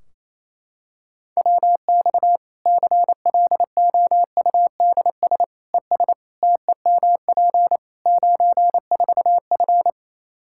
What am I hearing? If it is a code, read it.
Morse code, heard as a keyed tone, about 28 wpm, 700 Hz. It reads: WX CLOUDS ES TEMP 94F
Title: Practice QSO - Exchange 2 - Info - 28wpm